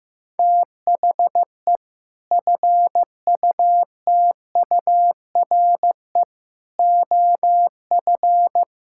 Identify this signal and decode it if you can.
Morse code, heard as a keyed tone, about 15 wpm, 700 Hz. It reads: THE FUTURE OF
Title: Sentences from Top 1000 Words - Review Part 6 - 15wpm